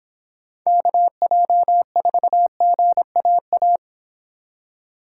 Morse code, heard as a keyed tone, about 26 wpm, 700 Hz. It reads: KJ4GAA